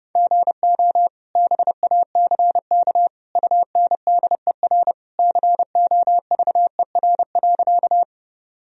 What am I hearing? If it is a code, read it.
Morse code, heard as a keyed tone, about 31 wpm, 700 Hz. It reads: GO BACK UNDER COVER.
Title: Sentences from Top 1000 Words - Review Part 1 - 30wpm